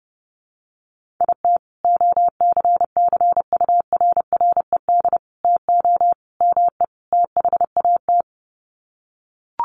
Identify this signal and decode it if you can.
Morse code, heard as a keyed tone, about 30 wpm, 700 Hz. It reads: IT OCCURRED TO ME THAT
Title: Sets of 5 Words Encore 30wpm